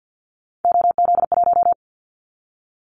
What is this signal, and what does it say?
Morse code, heard as a keyed tone, about 50 wpm, 700 Hz. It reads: OZ1
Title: Callsign Prefixes for Top 50 DXCC - 50wpm